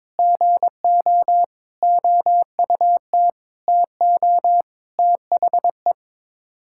Morse code, heard as a keyed tone, about 22 wpm, 700 Hz. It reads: GO OUT TO THE
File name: practice / 22wpm / Sets of 8 Words - 22wpm